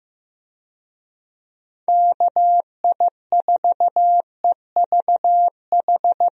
Morse code, heard as a keyed tone, about 15 wpm, 700 Hz. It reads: KI4EVH
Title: USA Call Signs - 6 Characters - Repeated 3x - 15wpm